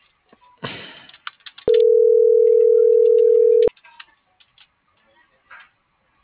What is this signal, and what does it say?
Ringback tone (North America), 440 + 480 Hz.